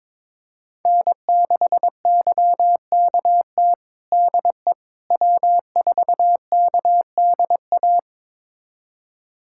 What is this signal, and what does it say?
Morse code, heard as a keyed tone, about 22 wpm, 700 Hz. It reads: N6YKT DE W4KDA
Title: USA Callsign DE Callsign - 5 Characters - 22wpm